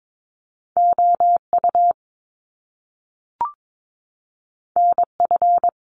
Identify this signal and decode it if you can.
Morse code, heard as a keyed tone, about 22 wpm, 700 Hz. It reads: O U N F
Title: Top Two Letter Combinations in Words 22wpm